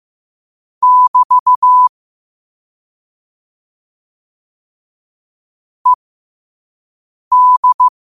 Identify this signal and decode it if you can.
Morse code, heard as a keyed tone, about 15 wpm, spaced as for 5 wpm, 1000 Hz. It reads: <BT> ED
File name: qotdfiles / 5wpm